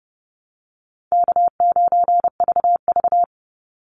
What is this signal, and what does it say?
Morse code, heard as a keyed tone, about 30 wpm, 700 Hz. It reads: K9VV
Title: Easy Call Signs - 30wpm